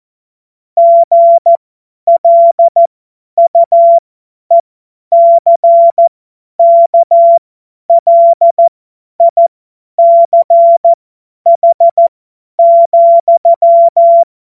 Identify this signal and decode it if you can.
Morse code, heard as a keyed tone, about 14 wpm, spaced as for 10 wpm, 680 Hz. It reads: GLUECKLICH,
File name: Morsekurs / Witze / Witz 05. Oktober 2021, 50 F70